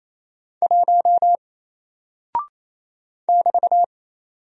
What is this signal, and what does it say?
Morse code, heard as a keyed tone, about 27 wpm, 700 Hz. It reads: 1 <BT>
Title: Single Character (Letter:Number:Punctuation:Prosign) Alt 28wpm